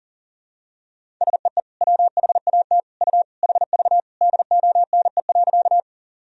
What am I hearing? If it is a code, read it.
Morse code, heard as a keyed tone, about 40 wpm, 700 Hz. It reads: SEE WHAT U HV DONE.
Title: Sentences from Top 500 Words and Abbreviations 40wpm